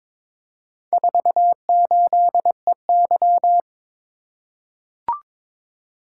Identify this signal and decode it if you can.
Morse code, heard as a keyed tone, about 22 wpm, 700 Hz. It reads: 48EY
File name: practice / 22wpm / Four Letter-Number Combinations - 22wpm